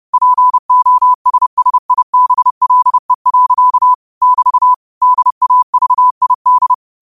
Morse code, heard as a keyed tone, about 30 wpm, 1000 Hz. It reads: POSSIBLE. <BT> DAVID
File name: qotdfiles / 30wpm